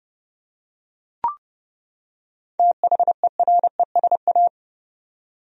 Morse code, heard as a keyed tone, about 31 wpm, 700 Hz. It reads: THERESA
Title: USA Common Names - 30wpm